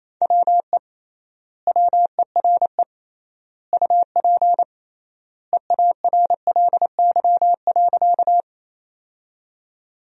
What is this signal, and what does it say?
Morse code, heard as a keyed tone, about 28 wpm, 700 Hz. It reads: WE WERE UP EARLY.
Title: 3x Word-Spacing - Sentences from Top 400 Words - 28wpm